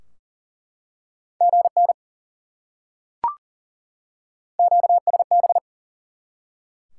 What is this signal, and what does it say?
Morse code, heard as a keyed tone, about 40 wpm, 700 Hz. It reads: GN QSB
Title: Top 90 QSO Elements 40wpm